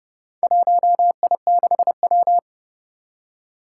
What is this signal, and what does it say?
Morse code, heard as a keyed tone, about 30 wpm, 700 Hz. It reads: 1I6W
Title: Four Challenging Letter-Number Combinations 30wpm